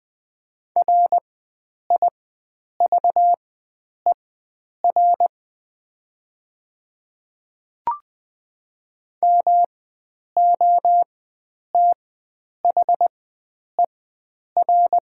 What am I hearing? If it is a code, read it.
Morse code, heard as a keyed tone, about 20 wpm, spaced as for 9 wpm, 700 Hz. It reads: RIVER MOTHER
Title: 4x Character Spacing - Top 300 Words - 20wpm